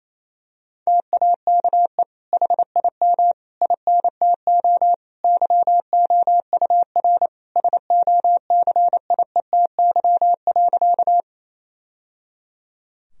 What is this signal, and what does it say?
Morse code, heard as a keyed tone, about 28 wpm, 700 Hz. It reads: TAKE HIM INTO YOUR SOCIETY.
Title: Sentences from Top 1000 Words - Review Part 4 - 28wpm